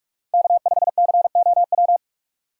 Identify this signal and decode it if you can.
Morse code, heard as a keyed tone, about 44 wpm, 700 Hz. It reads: K5COW
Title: USA Call Signs - 2x Character Spacing - 45wpm